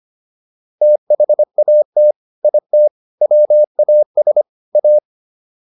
Morse code, heard as a keyed tone, about 25 wpm, 600 Hz. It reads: THAT IT WAS A